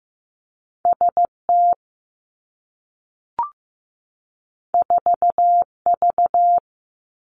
Morse code, heard as a keyed tone, about 15 wpm, 700 Hz. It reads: S T 4 V
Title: Two Challenging Letter-Number Combinations 15wpm